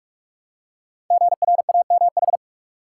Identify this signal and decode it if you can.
Morse code, heard as a keyed tone, about 44 wpm, 690 Hz. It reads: GRAMH
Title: Code Groups - 5 Letters - 45wpm